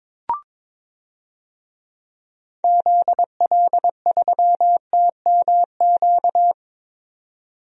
Morse code, heard as a keyed tone, about 21 wpm, 700 Hz. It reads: ZL3TMQ
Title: DXCC-100 Call Signs - 6 Characters - 22wpm